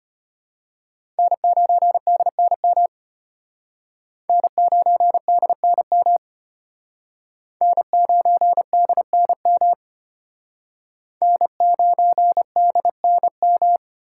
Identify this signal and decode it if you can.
Morse code, heard as a keyed tone, about 32 wpm, 700 Hz. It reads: N9DNM N9DNM N9DNM N9DNM